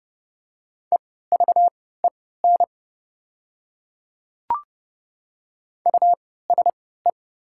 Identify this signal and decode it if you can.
Morse code, heard as a keyed tone, about 30 wpm, spaced as for 17 wpm, 700 Hz. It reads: EVEN USE
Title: 3x Character Spacing - Top 500 Words - 30wpm